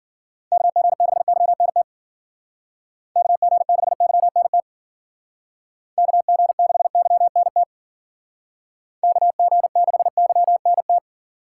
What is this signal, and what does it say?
Morse code, heard as a keyed tone, about 52 wpm, 700 Hz. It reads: KG6YNT KG6YNT KG6YNT KG6YNT